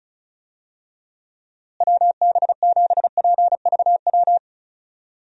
Morse code, heard as a keyed tone, about 36 wpm, 700 Hz. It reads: WB7PVW
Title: USA Call Signs - 6 Characters - Repeated 2x - 35wpm